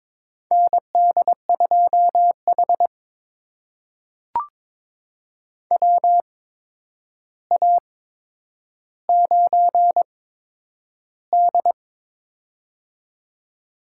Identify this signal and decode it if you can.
Morse code, heard as a keyed tone, about 22 wpm, spaced as for 9 wpm, 700 Hz. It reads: ND2H WA9D
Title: USA Call Signs - 4 Characters - 8x Character Spacing - 22wpm